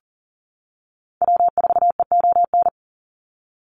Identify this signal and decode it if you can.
Morse code, heard as a keyed tone, about 39 wpm, 700 Hz. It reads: W4EON
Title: NA SOTA Activator Callsigns 40wpm